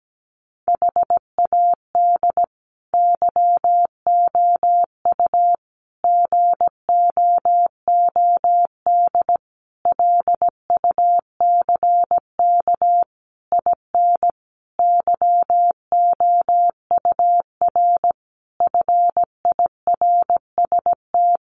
Morse code, heard as a keyed tone, about 17 wpm, 700 Hz. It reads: HAD YOU GOOD LUCK IN YOUR FIRST
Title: Sentences from Top 1500 Words 17wpm